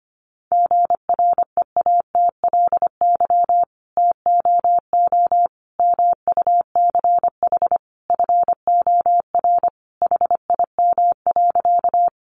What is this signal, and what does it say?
Morse code, heard as a keyed tone, about 25 wpm, 700 Hz. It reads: GREATLY TOO MUCH FOR HIM.
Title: Sentences from Top 1400 Words 25wpm